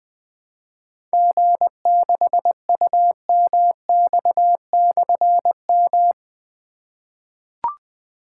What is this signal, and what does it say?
Morse code, heard as a keyed tone, about 20 wpm, 700 Hz. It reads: G6UMX/M
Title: Difficult Call Signs 20wpm